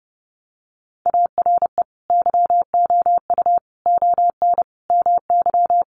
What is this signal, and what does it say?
Morse code, heard as a keyed tone, about 30 wpm, 700 Hz. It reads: ARE YOU ON MY
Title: Sentences from Top 1500 Words 30wpm